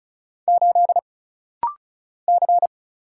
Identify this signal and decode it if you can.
Morse code, heard as a keyed tone, about 36 wpm, 700 Hz. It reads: 8 C
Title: Single Letter-Number - Rapid-Fire 35wpm